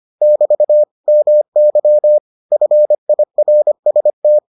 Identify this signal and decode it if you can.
Morse code, heard as a keyed tone, about 25 wpm, 600 Hz. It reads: <BT> MY FIRST